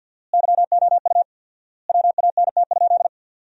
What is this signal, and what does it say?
Morse code, heard as a keyed tone, about 51 wpm, 700 Hz. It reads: YOU WANT?
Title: 4x Word-Spacing - Sentences from Top 200 Words - 50wpm